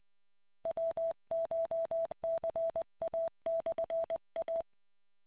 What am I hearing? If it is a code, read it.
Morse code, heard as a keyed tone, about 20 wpm, 660 Hz. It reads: WCHCA/A